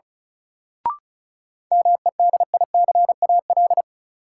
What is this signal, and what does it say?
Morse code, heard as a keyed tone, about 34 wpm, 700 Hz. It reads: MEDICAL
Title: Sentences from Top 800 Words - New Word Drill - 35wpm